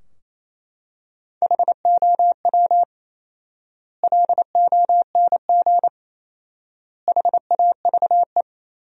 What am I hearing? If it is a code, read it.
Morse code, heard as a keyed tone, about 28 wpm, 700 Hz. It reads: HOW LONG HAVE
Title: 4x Word-Spacing - Sentences from Top 200 Words - 28wpm